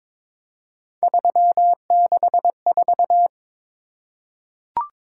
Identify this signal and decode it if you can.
Morse code, heard as a keyed tone, about 22 wpm, 700 Hz. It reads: 364
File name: practice / 22wpm / Three Letter-Number Combinations - 22wpm